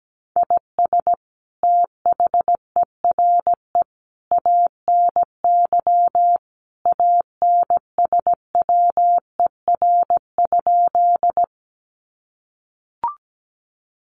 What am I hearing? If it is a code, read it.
Morse code, heard as a keyed tone, about 17 wpm, 700 Hz. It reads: IS THERE ANY ANSWER?
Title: Sentences from Top 300 Words 17wpm